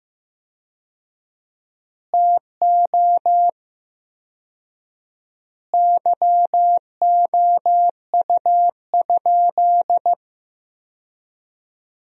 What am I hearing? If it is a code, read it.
Morse code, heard as a keyed tone, about 15 wpm, 700 Hz. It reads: TO YOU?
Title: 4x Word-Spacing - Sentences from Top 300 Words - 15wpm